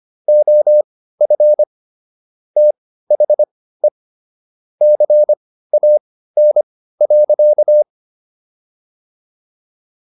Morse code, heard as a keyed tone, about 25 wpm, spaced as for 15 wpm, 600 Hz. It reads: OF THE CAN.